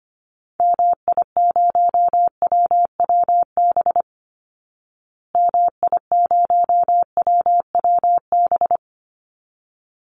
Morse code, heard as a keyed tone, about 25 wpm, 700 Hz. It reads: MI0WWB MI0WWB
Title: Call Signs Repeated 3x - 25wpm